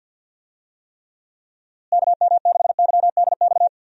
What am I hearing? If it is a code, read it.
Morse code, heard as a keyed tone, about 47 wpm, 700 Hz. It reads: KM6YDX
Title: USA Call Signs - 6 Characters - Repeated 3x - 50wpm